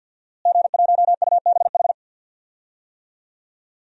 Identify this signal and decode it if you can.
Morse code, heard as a keyed tone, about 52 wpm, 700 Hz. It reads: G1UBH
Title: DXCC-100 Call Signs - 5 Characters - Repeated 3x - 50wpm